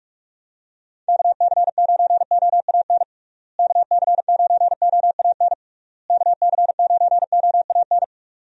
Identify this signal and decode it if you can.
Morse code, heard as a keyed tone, about 42 wpm, 700 Hz. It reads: KC9OAN KC9OAN KC9OAN
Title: USA Call Signs - 6 Characters - Repeated 3x - 45wpm